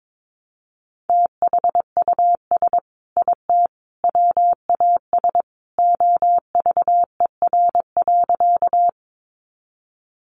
Morse code, heard as a keyed tone, about 22 wpm, 700 Hz. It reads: THUS IT WAS OVER.
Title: Sentences from Top 800 Words 22wpm